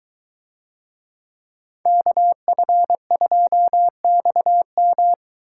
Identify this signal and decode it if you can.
Morse code, heard as a keyed tone, about 22 wpm, 700 Hz. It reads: KF2XM